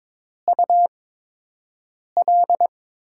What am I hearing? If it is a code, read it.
Morse code, heard as a keyed tone, about 22 wpm, 700 Hz. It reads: U L
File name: Top 90 QSO Elements - 8x Character Spacing - 22wpm